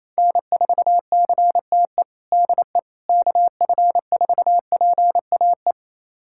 Morse code, heard as a keyed tone, about 28 wpm, 700 Hz. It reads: N4CTE DE KF4PAE
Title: USA Callsign DE Callsign 28wpm